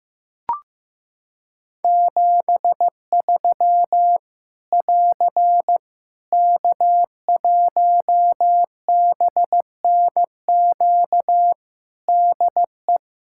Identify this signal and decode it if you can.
Morse code, heard as a keyed tone, about 16 wpm, 700 Hz. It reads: 73 <AR> K1BNQ DE
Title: Practice QSO - Exchange 4 - Ending - 15wpm